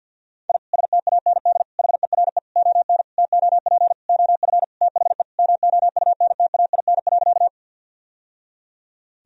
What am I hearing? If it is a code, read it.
Morse code, heard as a keyed tone, about 52 wpm, 700 Hz. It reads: I STAND HERE ON TOP OF THE MOUNTAIN.
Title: Sentences from Top 1000 Words - Review Part 1 - 50wpm